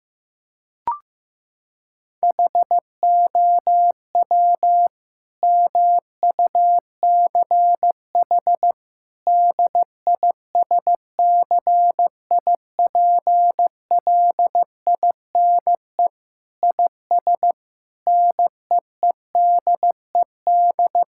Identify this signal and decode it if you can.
Morse code, heard as a keyed tone, about 16 wpm, 700 Hz. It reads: HOW MUCH DISCIPLINE IS NEEDED
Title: Sentences from Top 1000 Words - Review Part 6 - 15wpm